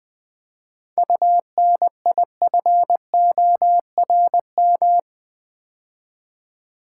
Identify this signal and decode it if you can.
Morse code, heard as a keyed tone, about 20 wpm, 700 Hz. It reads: UNIFORM
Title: 7 Letter Words - First 3 Letters Given - 20wpm